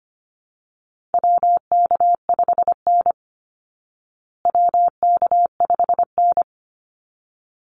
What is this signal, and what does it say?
Morse code, heard as a keyed tone, about 25 wpm, 700 Hz. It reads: WK5N WK5N
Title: Call Signs Repeated 3x - 25wpm